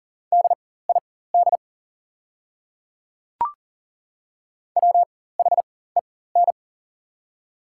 Morse code, heard as a keyed tone, about 38 wpm, spaced as for 18 wpm, 700 Hz. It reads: DID WHEN
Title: 4x Character Spacing - Top 500 Words - 40wpm